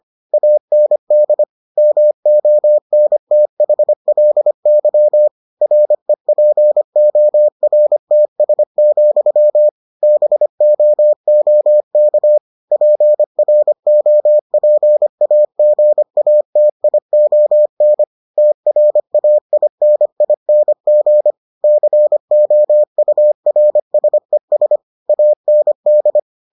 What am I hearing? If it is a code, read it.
Morse code, heard as a keyed tone, about 25 wpm, 600 Hz. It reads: AND MONTHLY REPORTS, BOOK PROPAGATION TRAINING COURSES AND